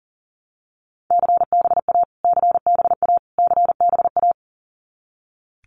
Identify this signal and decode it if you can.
Morse code, heard as a keyed tone, about 40 wpm, 700 Hz. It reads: CBA CBA CBA
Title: Abbreviations Repeated 3x - 40wpm